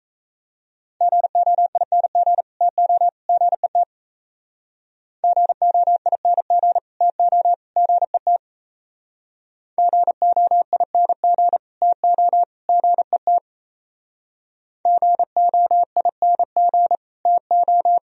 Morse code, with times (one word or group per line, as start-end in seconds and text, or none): 1.0-2.4 GOING
2.6-3.1 TO
3.3-3.8 GET
5.2-6.8 GOING
7.0-7.5 TO
7.8-8.4 GET
9.8-11.6 GOING
11.8-12.4 TO
12.7-13.4 GET
14.9-17.0 GOING
17.3-18.0 TO